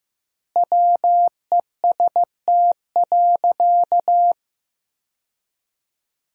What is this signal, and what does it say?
Morse code, heard as a keyed tone, about 15 wpm, 700 Hz. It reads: WEST.
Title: 2x Word-Spacing - Sentences from Top 400 Words - 15wpm